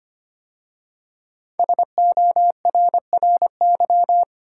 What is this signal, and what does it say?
Morse code, heard as a keyed tone, about 25 wpm, 700 Hz. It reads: SORRY
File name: 5 Letter Words - First 3 Letters Given - 25wpm